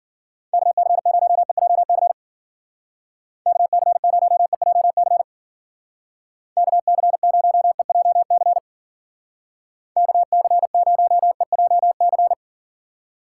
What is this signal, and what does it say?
Morse code, heard as a keyed tone, about 48 wpm, 700 Hz. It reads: KC0EJC KC0EJC KC0EJC KC0EJC
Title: Speed Racing - USA 2x3 Call Signs 40wpm